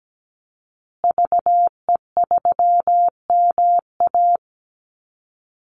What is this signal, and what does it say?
Morse code, heard as a keyed tone, about 17 wpm, 700 Hz. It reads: VE3MA
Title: Call Signs Repeated 3x - 17wpm